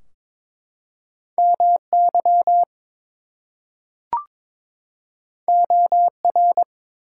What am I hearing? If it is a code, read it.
Morse code, heard as a keyed tone, about 22 wpm, 700 Hz. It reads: MY OR
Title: Two Letter Words - 22wpm